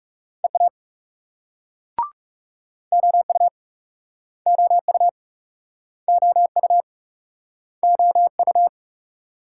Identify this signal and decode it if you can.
Morse code, heard as a keyed tone, about 43 wpm, 700 Hz. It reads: EA OU OU OU OU